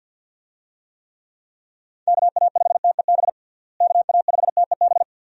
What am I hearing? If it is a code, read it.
Morse code, heard as a keyed tone, about 46 wpm, 700 Hz. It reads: KA5TEB KA5TEB
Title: USA Call Signs - 6 Characters - Repeated 2x - 50wpm